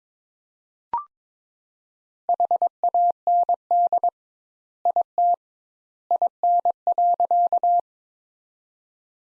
Morse code, heard as a keyed tone, about 22 wpm, 700 Hz. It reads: HAND IT IN.
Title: Sentences Top 200 Words - 22wpm